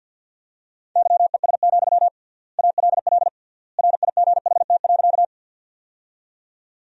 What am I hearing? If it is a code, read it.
Morse code, heard as a keyed tone, about 54 wpm, 700 Hz. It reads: YES, ALL RIGHT.
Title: 3x Word-Spacing - Sentences from Top 500 Words - 50wpm